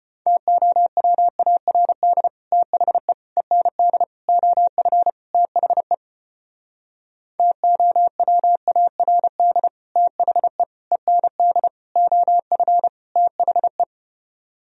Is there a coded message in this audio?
Yes: Morse code from 0.3 to 13.8 s, heard as a keyed tone, about 33 wpm, 700 Hz. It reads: TOWARD THE END OF THE TOWARD THE END OF THE